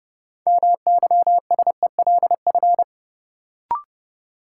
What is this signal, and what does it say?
Morse code, heard as a keyed tone, about 32 wpm, 700 Hz. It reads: MYSELF
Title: Sentences from Top 1300 Words - New Word Drill - 30wpm